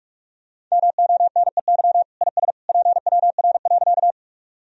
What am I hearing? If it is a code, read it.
Morse code, heard as a keyed tone, about 46 wpm, 700 Hz. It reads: MONEY IS PWR.
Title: Sentences from Top 500 Words and Abbreviations 45wpm